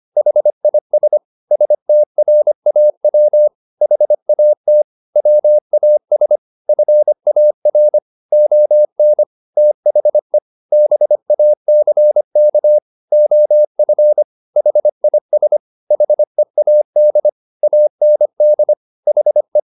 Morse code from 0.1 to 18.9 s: HIS STRAW HAT WAS FAR ON THE BACK OF HIS HEAD AND

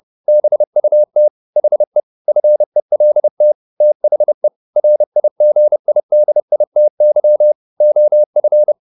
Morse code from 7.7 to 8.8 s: OF

